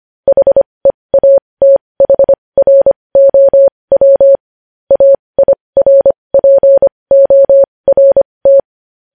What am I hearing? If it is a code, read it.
Morse code, heard as a keyed tone, about 25 wpm, spaced as for 20 wpm, 550 Hz. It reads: HEATHROW AIRPORT
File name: podcast / News headlines-1x-25x20